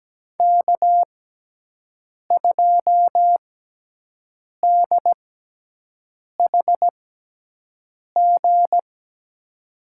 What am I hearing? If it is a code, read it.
Morse code, heard as a keyed tone, about 17 wpm, spaced as for 6 wpm, 700 Hz. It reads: K2DHG
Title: USA Call Signs - 5 Characters - 6x Character Spacing - 17wpm